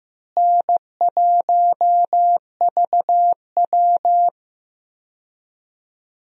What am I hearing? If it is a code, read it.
Morse code, heard as a keyed tone, about 15 wpm, 700 Hz. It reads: N1VW
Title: USA Call Signs - 4 Characters - Repeated 2x - 15wpm